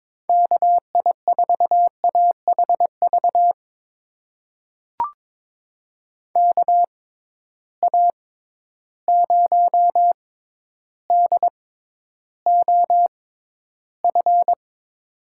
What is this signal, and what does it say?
Morse code, heard as a keyed tone, about 22 wpm, spaced as for 11 wpm, 700 Hz. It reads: KI4AHV KA0DOF
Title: USA Call Signs - 6 Characters - 6x Character Spacing - 22wpm